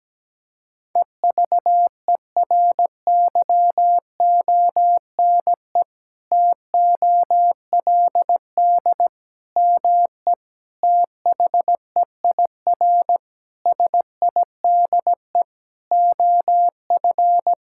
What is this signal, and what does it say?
Morse code, heard as a keyed tone, about 17 wpm, 700 Hz. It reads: EVERYONE TOLD ME THEIR SIDE OF
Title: Sentences from Top 1000 Words - Review Part 5 - 17wpm